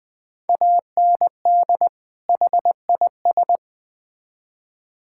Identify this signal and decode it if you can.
Morse code, heard as a keyed tone, about 21 wpm, 700 Hz. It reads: AND HIS
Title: Build A Phrase - 2-Word Phrases - 20wpm